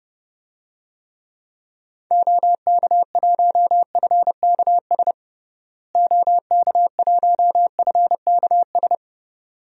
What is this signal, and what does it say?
Morse code, heard as a keyed tone, about 31 wpm, 700 Hz. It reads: OK1FKS OK1FKS
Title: DXCC-100 Call Signs - 6 Characters - Repeated 2x - 30wpm